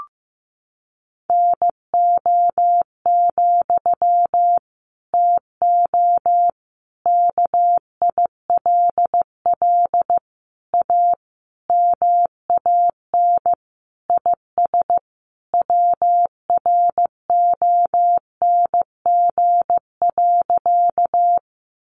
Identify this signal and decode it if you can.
Morse code, heard as a keyed tone, about 15 wpm, 700 Hz. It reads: NO, TO KILL A MAN IS WRONG.
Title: Sentences from Top 1400 Words 15wpm